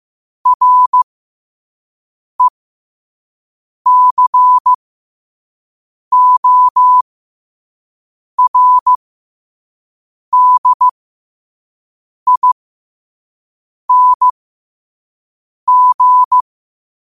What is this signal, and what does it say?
Morse code, heard as a keyed tone, about 15 wpm, spaced as for 5 wpm, 1000 Hz. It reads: RECORDING
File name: 5wpm